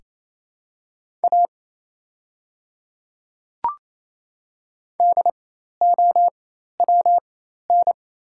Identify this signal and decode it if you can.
Morse code, heard as a keyed tone, about 28 wpm, spaced as for 13 wpm, 700 Hz. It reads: A DOWN
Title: 4x Character Spacing - Top 100 Words - 28wpm